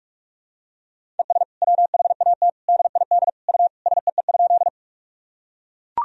Morse code, heard as a keyed tone, about 42 wpm, 700 Hz. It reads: ES WHAT DID U SEE?
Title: Sentences from Top 100 Words and Abbreviations 45wpm